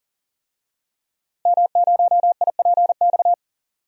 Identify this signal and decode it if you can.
Morse code, heard as a keyed tone, about 40 wpm, 700 Hz. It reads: M0IPX
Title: DXCC-100 Call Signs - 5 Characters - Repeated 2x - 40wpm